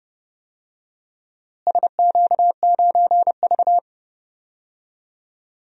Morse code, heard as a keyed tone, about 31 wpm, 700 Hz. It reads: SQ9V
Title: DXCC-100 Call Signs - 4 Characters - Repeated 2x - 30wpm